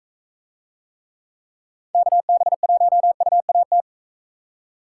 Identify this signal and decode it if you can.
Morse code, heard as a keyed tone, about 44 wpm, 700 Hz. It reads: KB1UAT